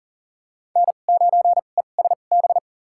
Morse code, heard as a keyed tone, about 40 wpm, spaced as for 29 wpm, 700 Hz. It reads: N9ESB
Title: USA Call Signs - 2x Character Spacing - 40wpm